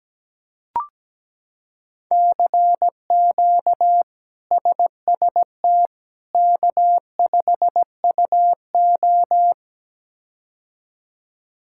Morse code, heard as a keyed tone, about 17 wpm, 700 Hz. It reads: CQ SST K5UO
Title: K1USN SST QSO Practice - 17wpm